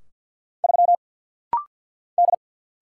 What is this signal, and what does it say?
Morse code, heard as a keyed tone, about 50 wpm, 690 Hz. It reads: 3 D
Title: Single Letter-Number - Rapid-Fire 50wpm